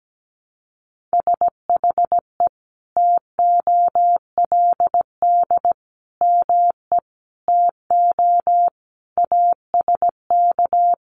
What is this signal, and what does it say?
Morse code, heard as a keyed tone, about 17 wpm, 700 Hz. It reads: SHE TOLD ME TO ASK
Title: Sentences from Top 400 Words 17wpm